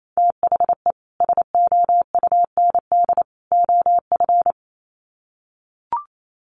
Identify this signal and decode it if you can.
Morse code, heard as a keyed tone, about 28 wpm, 700 Hz. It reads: THE SOUND OF
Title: Sets of 3 Words 28wpm